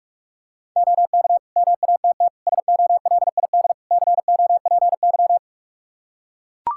Morse code, heard as a keyed tone, about 43 wpm, 700 Hz. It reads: OK MATT SOLID COPY
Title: Practice QSO - Exchange 3 - More Info - 45wpm